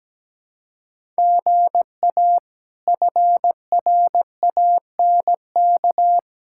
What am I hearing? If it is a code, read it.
Morse code, heard as a keyed tone, about 17 wpm, 700 Hz. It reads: GA FRANK
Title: K1USN SST QSO Practice - 17wpm